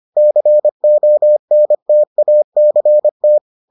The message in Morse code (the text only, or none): CONTACT